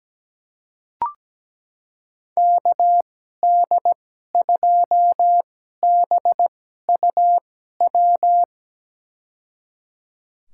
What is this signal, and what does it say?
Morse code, heard as a keyed tone, about 17 wpm, spaced as for 12 wpm, 700 Hz. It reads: KD2BUW
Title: USA Call Signs - 6 Characters - 2x Character Spacing - 17wpm